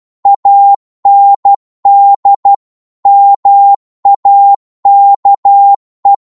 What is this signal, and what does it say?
Morse code, heard as a keyed tone, about 12 wpm, 800 Hz. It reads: AND MAKE
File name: cwmp3 / psg-1-12wpm